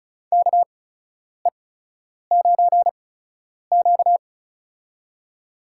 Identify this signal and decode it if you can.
Morse code, heard as a keyed tone, about 36 wpm, spaced as for 10 wpm, 700 Hz. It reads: KE9Q
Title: USA Call Signs - 8x Character Spacing - 35wpm